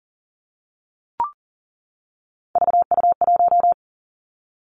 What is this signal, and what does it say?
Morse code, heard as a keyed tone, about 38 wpm, 690 Hz. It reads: VU1
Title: Three Challenging Letter-Number Combinations 40wpm